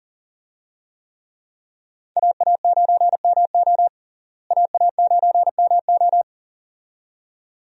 Morse code, heard as a keyed tone, about 40 wpm, 700 Hz. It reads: AA9MO AA9MO
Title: USA Call Signs - 5 Characters - Repeated 2x - 40wpm